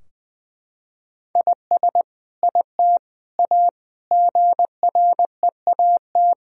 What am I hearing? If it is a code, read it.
Morse code, heard as a keyed tone, about 20 wpm, 700 Hz. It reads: IS IT A GREAT
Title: Sentences from Top 1000 Words - Review Part 4 - 20wpm